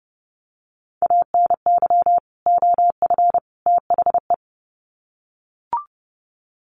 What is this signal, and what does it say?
Morse code, heard as a keyed tone, about 30 wpm, 700 Hz. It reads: ANY OF THE
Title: Sets of 3 Words 30wpm